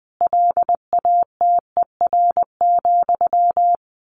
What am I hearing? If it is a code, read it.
Morse code, heard as a keyed tone, about 20 wpm, 700 Hz. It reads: LATER,
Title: A Sound Of Thunder 20wpm